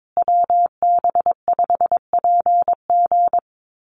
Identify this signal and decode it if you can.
Morse code, heard as a keyed tone, about 22 wpm, 700 Hz. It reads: WB5PG
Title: USA Call Signs Repeated 2x 22wpm